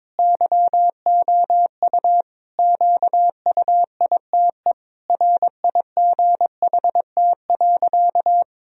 Morse code, heard as a keyed tone, about 22 wpm, 700 Hz. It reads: YOU QUITE RIGHT.
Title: Sentences from Top 1000 Words - Review Part 2 - 22wpm